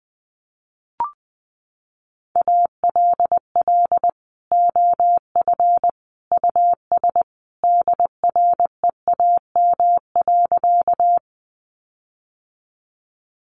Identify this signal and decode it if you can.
Morse code, heard as a keyed tone, about 19 wpm, 700 Hz. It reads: ALL OF US DREAM.
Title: Sentences from Top 1500 Words 20wpm